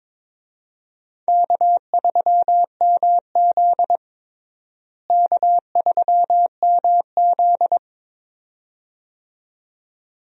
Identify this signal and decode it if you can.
Morse code, heard as a keyed tone, about 22 wpm, 700 Hz. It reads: K3MZ K3MZ
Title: USA Call Signs - 4 Characters - Repeated 2x - 22wpm